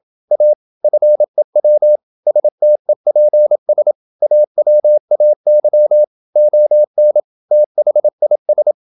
Morse code, heard as a keyed tone, about 27 wpm, 600 Hz. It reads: A FEW STEPS AWAY ON THIS
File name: Never 0027wpm